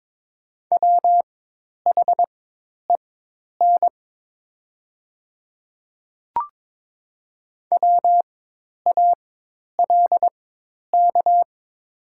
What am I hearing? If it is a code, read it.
Morse code, heard as a keyed tone, about 23 wpm, spaced as for 10 wpm, 700 Hz. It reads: WHEN WALK
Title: 4x Character Spacing - Top 300 Words - 22wpm